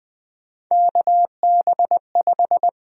Morse code, heard as a keyed tone, about 20 wpm, 700 Hz. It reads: KB5
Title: USA Call Sign Prefixes - Repeated 2x - 20wpm